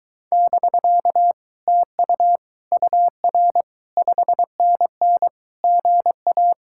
Morse code, heard as a keyed tone, about 23 wpm, 700 Hz. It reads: <BK> TU UR 5NN GA